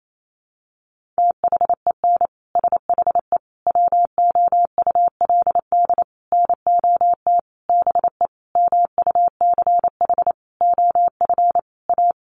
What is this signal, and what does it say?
Morse code, heard as a keyed tone, about 28 wpm, 700 Hz. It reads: THEN SHE WOULD NOT BE MUCH OF A
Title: Sentences from Top 1400 Words 28wpm